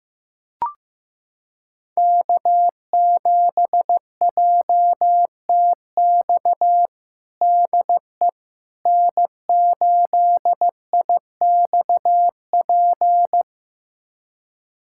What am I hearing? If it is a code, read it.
Morse code, heard as a keyed tone, about 15 wpm, 700 Hz. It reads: K7JTX DE N8IXP
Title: USA Callsign DE Callsign - 5 Characters - 15wpm